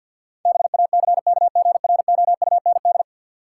Morse code, heard as a keyed tone, about 52 wpm, 700 Hz. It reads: BACKGROUND
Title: Sentences from Top 1300 Words - New Word Drill Alt - 50wpm